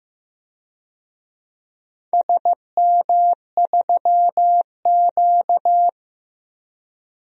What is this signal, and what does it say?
Morse code, heard as a keyed tone, about 15 wpm, 700 Hz. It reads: SM3Q
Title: DXCC-100 Call Signs - 4 Characters - 15wpm